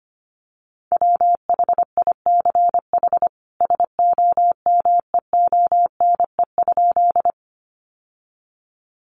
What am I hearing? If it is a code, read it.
Morse code, heard as a keyed tone, about 25 wpm, 700 Hz. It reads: WHICH SOMEONE?
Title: Sentences from Top 600 Words 25wpm